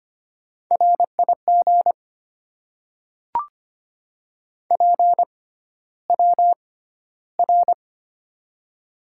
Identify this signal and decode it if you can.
Morse code, heard as a keyed tone, about 25 wpm, spaced as for 13 wpm, 700 Hz. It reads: RIG PWR
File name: Top 90 QSO Elements - 6x Character Spacing - 25wpm